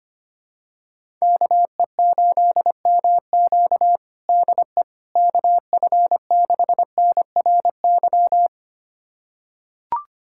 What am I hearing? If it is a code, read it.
Morse code, heard as a keyed tone, about 25 wpm, 700 Hz. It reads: KE8MQ DE KF6NRY
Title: USA Callsign DE Callsign 25wpm